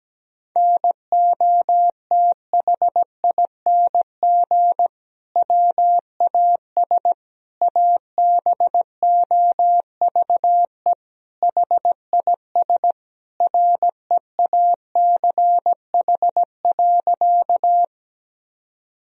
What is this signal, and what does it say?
Morse code, heard as a keyed tone, about 17 wpm, 700 Hz. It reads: NOTHING WAS ABOVE HIS REACH.
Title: Sentences from Top 1000 Words - Review Part 2 - 17wpm